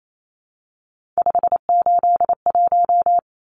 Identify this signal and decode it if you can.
Morse code, heard as a keyed tone, about 28 wpm, 700 Hz. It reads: 581
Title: Numbers - 3-Digit 28wpm